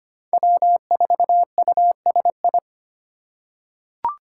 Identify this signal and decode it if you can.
Morse code, heard as a keyed tone, about 25 wpm, 700 Hz. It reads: W4USI
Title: K1USN SST QSO Practice - 25wpm